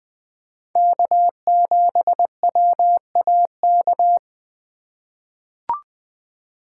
Morse code, heard as a keyed tone, about 20 wpm, 700 Hz. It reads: K7WAK
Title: USA Call Signs - 3x Character Spacing - 20wpm